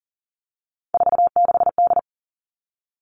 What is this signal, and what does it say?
Morse code, heard as a keyed tone, about 43 wpm, 700 Hz. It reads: 46D
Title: Three Challenging Letter-Number Combinations 40wpm